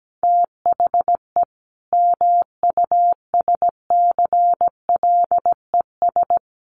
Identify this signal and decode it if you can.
Morse code, heard as a keyed tone, about 17 wpm, 700 Hz. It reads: THE MUSCLES
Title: A Sound Of Thunder 17wpm